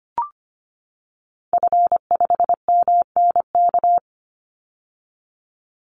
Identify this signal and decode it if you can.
Morse code, heard as a keyed tone, about 25 wpm, 700 Hz. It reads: F5MNK
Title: Call Signs - 25wpm